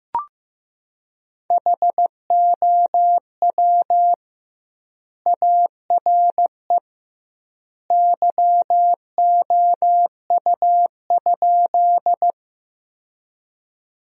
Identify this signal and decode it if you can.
Morse code, heard as a keyed tone, about 16 wpm, 700 Hz. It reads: HOW ARE YOU?
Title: 2x Word-Spacing - Sentences from Top 100 Words - 15wpm